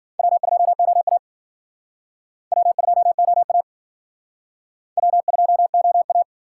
Morse code, heard as a keyed tone, about 52 wpm, 700 Hz. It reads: W2OA W2OA W2OA